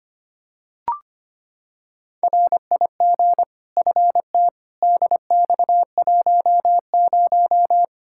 Morse code, heard as a keyed tone, about 25 wpm, 700 Hz. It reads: RIG FT DX10
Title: K1USN SST QSO Practice 25wpm